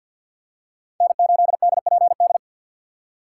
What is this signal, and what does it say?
Morse code, heard as a keyed tone, about 50 wpm, 700 Hz. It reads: N8DPD
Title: K1USN SST QSO Practice - 50wpm